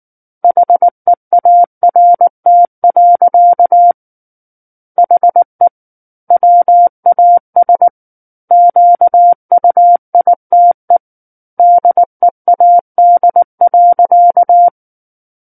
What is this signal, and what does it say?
Morse code, heard as a keyed tone, about 19 wpm, 700 Hz. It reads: HEART. HE WAS QUITE DEAD.
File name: Book One / 10-Chapter 10 - 19 WPM